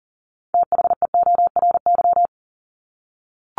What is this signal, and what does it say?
Morse code, heard as a keyed tone, about 40 wpm, 700 Hz. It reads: THEORY
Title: Top 501-750 Words 40wpm